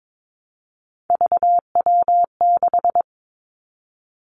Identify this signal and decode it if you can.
Morse code, heard as a keyed tone, about 22 wpm, 700 Hz. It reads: VW6
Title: Three Challenging Letter-Number Combinations 22wpm